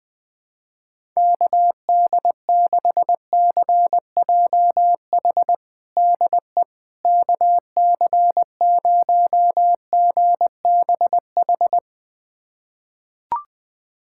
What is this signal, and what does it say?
Morse code, heard as a keyed tone, about 20 wpm, 690 Hz. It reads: KD6CJH DE KC0GBH
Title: USA Callsign DE Callsign 20wpm